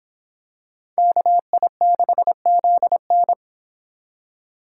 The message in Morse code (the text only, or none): KI6ZN